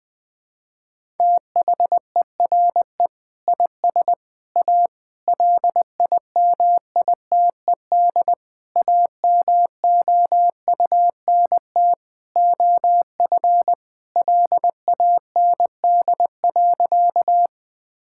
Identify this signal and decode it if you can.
Morse code, heard as a keyed tone, about 21 wpm, 700 Hz. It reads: THERE IS A LIMITED AMOUNT OF LAND.
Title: Sentences from Top 1000 Words - Review Part 6 - 20wpm